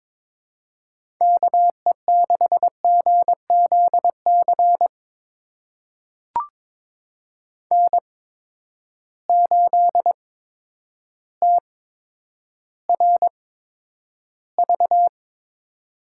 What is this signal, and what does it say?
Morse code, heard as a keyed tone, about 21 wpm, spaced as for 10 wpm, 700 Hz. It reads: KE6GZC N8TRV
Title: USA Call Signs - 8x Character Spacing - 22wpm